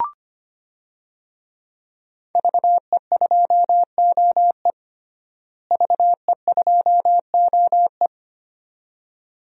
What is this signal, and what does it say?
Morse code, heard as a keyed tone, about 25 wpm, 700 Hz. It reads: VE2OE VE2OE
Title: DXCC-100 Call Signs - 5 Characters - Repeated 2x - 25wpm